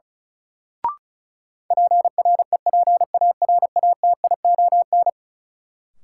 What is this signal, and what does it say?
Morse code, heard as a keyed tone, about 34 wpm, 700 Hz. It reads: PREPARATION
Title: Sentences from Top 1400 Words - New Word Drill - 35wpm